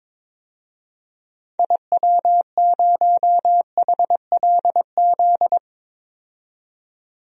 Morse code, heard as a keyed tone, about 22 wpm, 700 Hz. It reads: IW0HLZ
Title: DXCC-100 Call Signs - 6 Characters - 22wpm